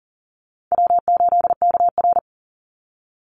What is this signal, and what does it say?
Morse code, heard as a keyed tone, about 40 wpm, 700 Hz. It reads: W8KR
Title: Easy Call Signs Repeated 3x - 40wpm